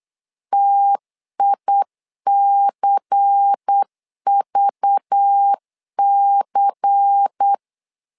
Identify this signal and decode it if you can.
Morse code, heard as a keyed tone, about 8 wpm, 800 Hz. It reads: TICVC
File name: LEZIONI / 5 LBVC / 5 LBVC 8wpm 2935